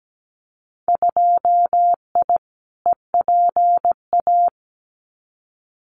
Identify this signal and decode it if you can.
Morse code, heard as a keyed tone, about 17 wpm, 700 Hz. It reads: 2I EPA
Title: Winter Field Day Exchanges 17wpm